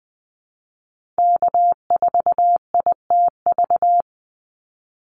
Morse code, heard as a keyed tone, about 20 wpm, 700 Hz. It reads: K4ITV
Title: Call Signs Repeated 3x - 20wpm